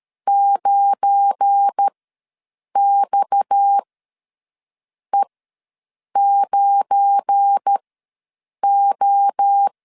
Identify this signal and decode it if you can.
Morse code, heard as a keyed tone, about 13 wpm, spaced as for 6 wpm, 800 Hz. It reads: 9XE9O
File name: LEZIONI / 7 GF9Z / 7 GF9 ws 12wpm 3344